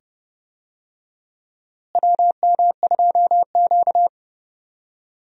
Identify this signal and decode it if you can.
Morse code, heard as a keyed tone, about 30 wpm, 700 Hz. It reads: WM2Q
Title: USA Call Signs - 4 Characters - Repeated 2x - 30wpm